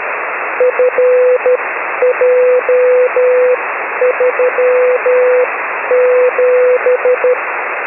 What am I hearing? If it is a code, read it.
Morse code, heard as a keyed tone, about 12 wpm, spaced as for 10 wpm, 500 Hz, with radio noise. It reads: FJ37
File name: 6m-hk6frc-b